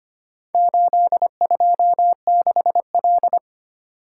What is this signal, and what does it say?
Morse code, heard as a keyed tone, about 25 wpm, 700 Hz. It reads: 826L
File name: practice / 25wpm / Four Letter-Number Combinations - 25wpm